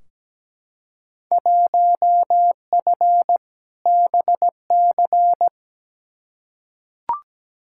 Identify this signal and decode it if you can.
Morse code, heard as a keyed tone, about 17 wpm, 700 Hz. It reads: 1F BC
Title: Field Day Exchanges Encore - 17wpm